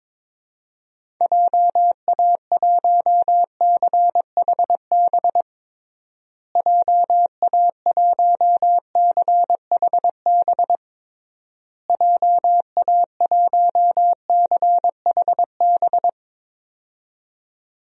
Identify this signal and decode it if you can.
Morse code, heard as a keyed tone, about 21 wpm, 700 Hz. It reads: JA1CHB JA1CHB JA1CHB
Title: DXCC-100 Call Signs - 6 Characters - Repeated 3x - 22wpm